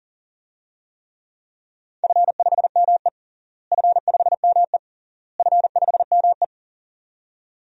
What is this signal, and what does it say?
Morse code, heard as a keyed tone, about 40 wpm, 700 Hz. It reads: F5ME F5ME F5ME
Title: DXCC-100 Call Signs - 4 Characters - Repeated 3x - 40wpm